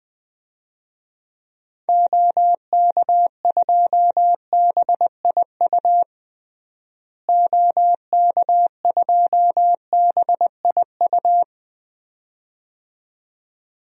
Morse code, heard as a keyed tone, about 20 wpm, 700 Hz. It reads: OK2BIU OK2BIU
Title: DXCC-100 Call Signs - 6 Characters - Repeated 2x - 20wpm